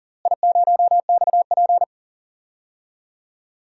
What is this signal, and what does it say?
Morse code, heard as a keyed tone, about 40 wpm, 700 Hz. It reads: I0XP
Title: DXCC-100 Call Signs - 4 Characters - Repeated 3x - 40wpm